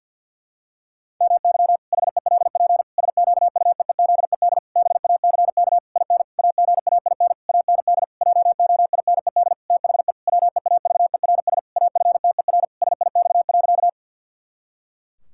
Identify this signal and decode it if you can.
Morse code, heard as a keyed tone, about 50 wpm, 700 Hz. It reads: MY HELP SQUEEZED BACK IN AGAIN AND JOINED THE WEAVERS AFTER SIX.
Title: Pangram Challenge v2 - 50wpm